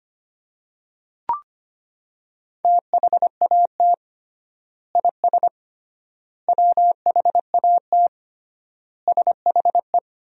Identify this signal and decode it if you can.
Morse code, heard as a keyed tone, about 25 wpm, 700 Hz. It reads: THAT IS WHAT SHE
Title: 3x Word-Spacing - Sentences from Top 100 Words - 25wpm